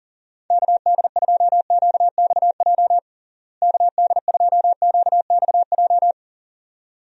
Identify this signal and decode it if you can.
Morse code, heard as a keyed tone, about 37 wpm, 700 Hz. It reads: KD2QXJ KD2QXJ
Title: USA Call Signs - 6 Characters - Repeated 2x - 40wpm